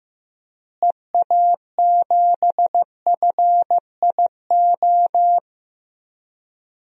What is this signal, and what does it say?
Morse code, heard as a keyed tone, about 15 wpm, 700 Hz. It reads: EA7FIO
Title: DXCC-100 Call Signs - 6 Characters - Repeated 2x - 15wpm